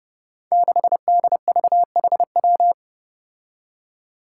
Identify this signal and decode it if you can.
Morse code, heard as a keyed tone, about 30 wpm, 700 Hz. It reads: 6DVHW
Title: Five Challenging Letter-Number Combinations 30wpm